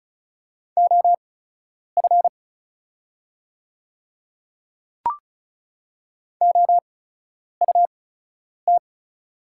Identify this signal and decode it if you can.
Morse code, heard as a keyed tone, about 36 wpm, spaced as for 10 wpm, 690 Hz. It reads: OF OUT
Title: 8x Character Spacing - Top 100 Words - 35wpm